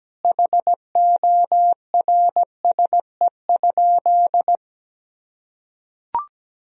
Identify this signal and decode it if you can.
Morse code, heard as a keyed tone, about 17 wpm, 700 Hz. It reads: HORSE?
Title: Sentences from Top 500 Words and Abbreviations 17wpm